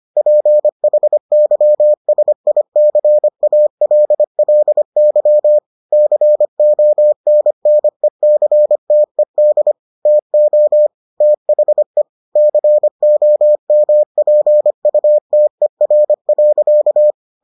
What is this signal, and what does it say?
Morse code, heard as a keyed tone, about 25 wpm, 600 Hz. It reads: PHYSICALLY CONNECTED TO THE COMPUTER.